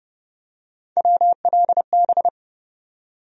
Morse code, heard as a keyed tone, about 29 wpm, 700 Hz. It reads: WLB
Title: USA Call Sign Postfixes - 3 Letters 30wpm